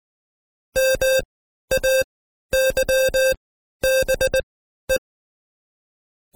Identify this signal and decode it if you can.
Morse code, heard as a keyed tone, about 19 wpm, spaced as for 12 wpm, 530 Hz. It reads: MAYBE
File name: Code Quick / AudioDownld2 / Seg132-Stepper5 11WPM